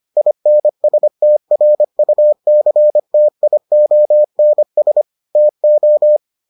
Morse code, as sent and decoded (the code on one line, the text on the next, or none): .. -. ... - .-. ..- -.-. - .. --- -. ... / - ---
INSTRUCTIONS TO